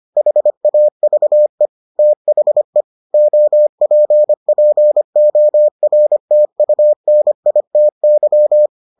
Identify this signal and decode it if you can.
Morse code, heard as a keyed tone, about 25 wpm, 600 Hz. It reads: HAVE THE OPPORTUNITY